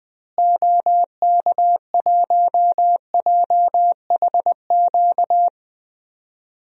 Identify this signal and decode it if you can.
Morse code, heard as a keyed tone, about 20 wpm, 700 Hz. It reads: OK1JHQ
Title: DXCC-100 Call Signs - 6 Characters - Repeated 2x - 20wpm